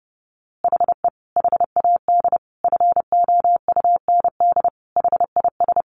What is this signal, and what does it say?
Morse code, heard as a keyed tone, about 30 wpm, 700 Hz. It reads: HE HAD FOUND HIS
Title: A Sound Of Thunder 30wpm